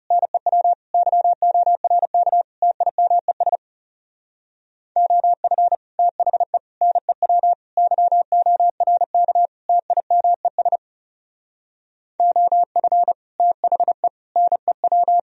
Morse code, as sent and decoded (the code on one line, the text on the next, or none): -. .-- / -.-- --- .-. -.- / - .. -- . ... / --- ..-. / - .... / -. .-- / -.-- --- .-. -.- / - .. -- . ... / --- ..-. / - .... / -. .--
NEW YORK TIMES OF THE NEW YORK TIMES OF THE NEW